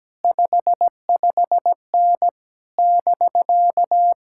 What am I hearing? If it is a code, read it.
Morse code, heard as a keyed tone, about 17 wpm, 700 Hz. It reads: 55N <BK>
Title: Speed-Racing Common QSO Phrases Encore - 17wpm